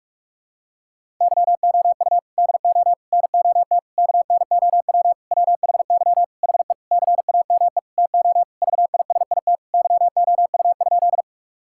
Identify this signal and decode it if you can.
Morse code, heard as a keyed tone, about 42 wpm, 700 Hz. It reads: YOU DO NOT KNOW WHY HE CAME TO VISIT YOU?
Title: Sentences from Top 1000 Words - Review Part 6 - 45wpm